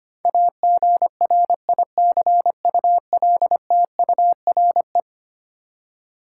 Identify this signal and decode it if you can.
Morse code, heard as a keyed tone, about 25 wpm, 700 Hz. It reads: AGRICULTURE
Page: morse-code-ninja-practice-25wpm